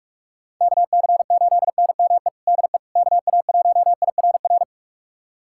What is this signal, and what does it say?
Morse code, heard as a keyed tone, about 43 wpm, 700 Hz. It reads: KC8NME DE KA1IRR
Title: USA Callsign DE Callsign - 6 Characters - 45wpm